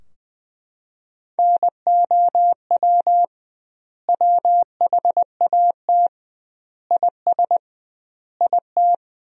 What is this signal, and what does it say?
Morse code, heard as a keyed tone, about 21 wpm, 700 Hz. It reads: NOW WHAT IS IT
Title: 2x Word-Spacing - Sentences from Top 200 Words - 20wpm